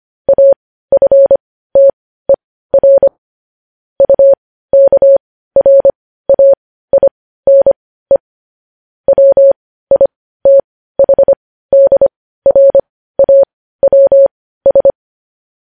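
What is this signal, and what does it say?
Morse code, heard as a keyed tone, about 25 wpm, spaced as for 15 wpm, 550 Hz. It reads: AFTER UKRAINE WITHDRAWS
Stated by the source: Morse code news headlines at 15 WPM.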